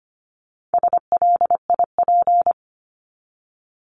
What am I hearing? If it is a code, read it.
Morse code, heard as a keyed tone, about 25 wpm, 700 Hz. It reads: SLIP
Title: Top 1751-2000 Words 25wpm